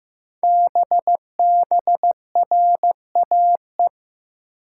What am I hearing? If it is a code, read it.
Morse code, heard as a keyed tone, about 15 wpm, 700 Hz. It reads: BBRAE
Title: Code Groups - 5 Letters - 15wpm